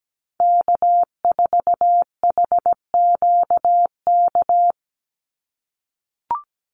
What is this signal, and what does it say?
Morse code, heard as a keyed tone, about 17 wpm, 700 Hz. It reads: K4HQK
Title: Call Signs - 17wpm